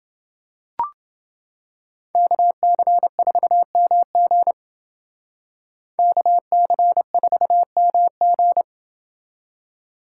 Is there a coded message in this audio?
Yes: Morse code from 2.1 to 8.6 s, heard as a keyed tone, about 29 wpm, 700 Hz. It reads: KC4MG KC4MG